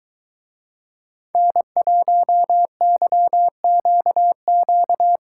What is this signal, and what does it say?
Morse code, heard as a keyed tone, about 23 wpm, 700 Hz. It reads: N1YQQ